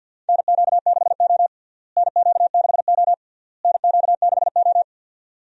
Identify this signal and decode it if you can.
Morse code, heard as a keyed tone, about 52 wpm, 700 Hz. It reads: NQ6O NQ6O NQ6O
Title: USA Call Signs - 4 Characters - Repeated 3x - 50wpm